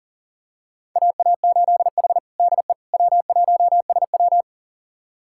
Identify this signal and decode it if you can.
Morse code, heard as a keyed tone, about 40 wpm, 700 Hz. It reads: AA8H DE W1SW
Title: USA Callsign DE Callsign - 4 Characters - 40wpm